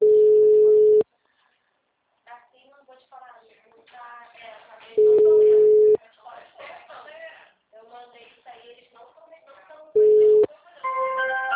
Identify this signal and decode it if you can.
Ringback tone (Europe, 425 Hz), with speech.
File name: out- -2033- - - .364